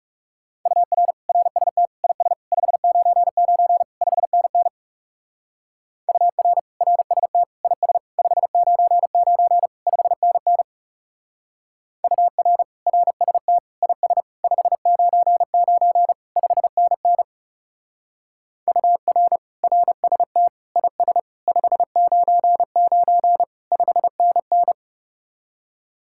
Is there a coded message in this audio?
Yes: Morse code from 0.7 to 24.7 s, heard as a keyed tone, about 39 wpm, 700 Hz. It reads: UR RST IS 599 5NN UR RST IS 599 5NN UR RST IS 599 5NN UR RST IS 599 5NN